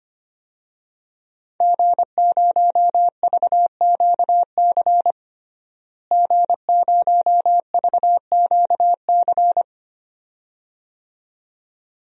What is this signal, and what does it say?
Morse code, heard as a keyed tone, about 25 wpm, 700 Hz. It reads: G0VQC G0VQC
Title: DXCC-100 Call Signs - 5 Characters - Repeated 2x - 25wpm